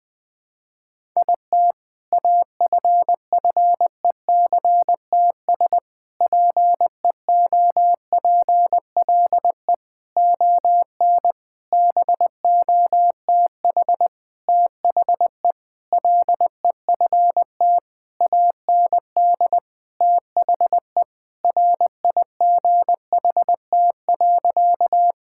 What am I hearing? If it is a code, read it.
Morse code, heard as a keyed tone, about 20 wpm, 700 Hz. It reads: IT AFFECTS PEOPLE ON BOTH THE LEFT AND THE RIGHT.
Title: Sentences with 7-Word Phrases - 20wpm